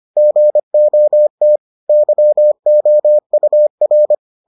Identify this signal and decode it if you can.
Morse code, heard as a keyed tone, about 25 wpm, 600 Hz. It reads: GOT YOUR